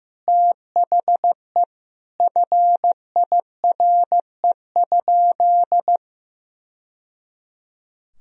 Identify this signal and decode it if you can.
Morse code, heard as a keyed tone, about 15 wpm, 700 Hz. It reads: THE FIRE?
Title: Sentences from Top 1000 Words - Review Part 3 - 15wpm